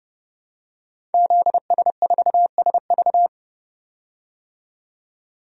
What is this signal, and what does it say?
Morse code, heard as a keyed tone, about 30 wpm, 700 Hz. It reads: ZS4SV
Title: DXCC-100 Call Signs - 5 Characters - 30wpm